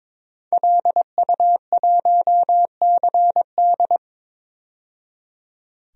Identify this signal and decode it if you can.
Morse code, heard as a keyed tone, about 22 wpm, 700 Hz. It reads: LU1CD
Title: DXCC-100 Call Signs - 5 Characters - 22wpm